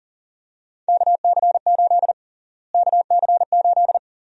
Morse code, heard as a keyed tone, about 40 wpm, 700 Hz. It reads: KC8 KC8
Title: USA Call Sign Prefixes - Repeated 2x - 40wpm